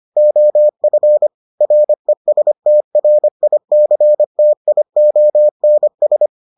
Morse code, heard as a keyed tone, about 25 wpm, 600 Hz. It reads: OF RESTRICTIONS